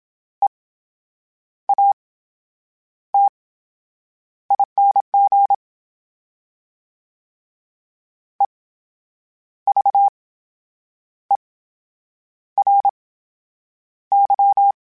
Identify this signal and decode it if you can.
Morse code, heard as a keyed tone, about 27 wpm, spaced as for 8 wpm, 800 Hz. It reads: EATING EVERY
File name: Special INGs / Special INGs-mp3s / VerySlowToFaster Ing @25